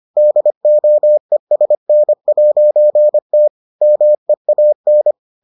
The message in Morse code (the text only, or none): DOESN'T MEAN